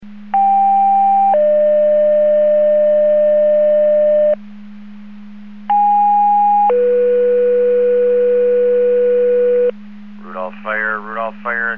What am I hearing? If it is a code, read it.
Two-tone page, heard as a one-second tone, then a three-second tone, about 790 then 600 Hz, about 820 then 490 Hz, with speech. It is heard from 0.3 to 9.7 s.